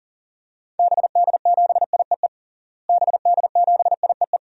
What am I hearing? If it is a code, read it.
Morse code, heard as a keyed tone, about 40 wpm, 700 Hz. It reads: BD7IEE BD7IEE
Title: DXCC-100 Call Signs - 6 Characters - Repeated 2x - 40wpm